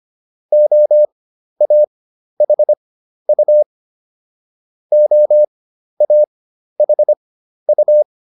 Morse code, heard as a keyed tone, about 25 wpm, spaced as for 12 wpm, 600 Hz. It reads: OAHU OAHU